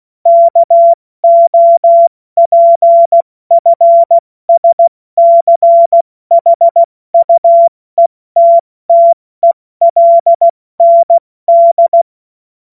Morse code, heard as a keyed tone, about 16 wpm, spaced as for 14 wpm, 680 Hz. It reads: KOPFSCHUETTELND
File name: Morsekurs / Witze / Witz 17. Mai 2022 70 F80